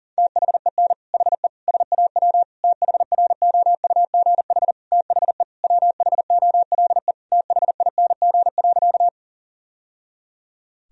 Morse code, heard as a keyed tone, about 40 wpm, 700 Hz. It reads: THEN HE SAW THROUGH THE WHOLE THING.
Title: Sentences from Top 1000 Words - Review Part 1 - 40wpm